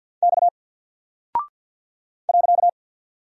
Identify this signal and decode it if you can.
Morse code, heard as a keyed tone, about 48 wpm, 700 Hz. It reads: X .
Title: Single Character (Letter:Number:Punctuation:Prosign) Alt 50wpm